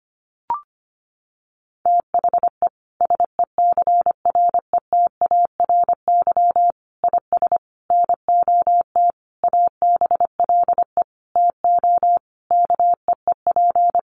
Morse code, heard as a keyed tone, about 25 wpm, 700 Hz. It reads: THE SECRETARY IS NOT ABLE TO KEEP
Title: Sentences from Top 1300 Words 25wpm